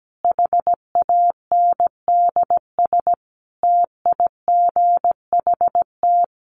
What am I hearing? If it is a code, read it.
Morse code, heard as a keyed tone, about 17 wpm, 700 Hz. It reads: HANDS TIGHT
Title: A Sound Of Thunder 17wpm